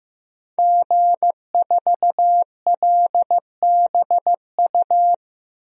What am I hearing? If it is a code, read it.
Morse code, heard as a keyed tone, about 15 wpm, 700 Hz. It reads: G4LBU
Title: DXCC-100 Call Signs - 5 Characters - Repeated 3x - 15wpm